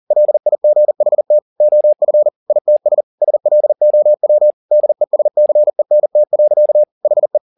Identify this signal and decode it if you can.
Morse code, heard as a keyed tone, about 40 wpm, 600 Hz. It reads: LIGHT OF ITS SLOW DESCENT. HE
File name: the old man and the sea 2 40wpm